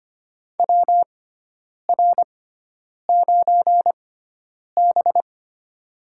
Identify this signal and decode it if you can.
Morse code, heard as a keyed tone, about 25 wpm, spaced as for 9 wpm, 700 Hz. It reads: WR9B
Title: USA Call Signs - 4 Characters - 6x Character Spacing - 25wpm